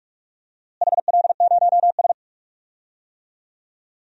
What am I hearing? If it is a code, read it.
Morse code, heard as a keyed tone, about 46 wpm, 700 Hz. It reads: HL0S
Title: DXCC-100 Call Signs - 4 Characters - Repeated 2x - 45wpm